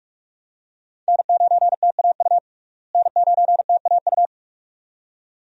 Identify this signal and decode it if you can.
Morse code, heard as a keyed tone, about 46 wpm, 700 Hz. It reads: N9TAU N9TAU
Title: USA Call Signs - 5 Characters - Repeated 2x - 45wpm